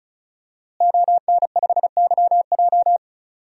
Morse code, heard as a keyed tone, about 34 wpm, 700 Hz. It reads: ON5YJ
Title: DXCC-100 Call Signs - 5 Characters - Repeated 3x - 35wpm